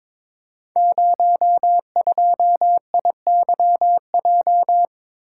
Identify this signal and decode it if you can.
Morse code, heard as a keyed tone, about 23 wpm, 690 Hz. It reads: 02IYJ
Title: Code Groups - 5 Letter:Numbers - 22wpm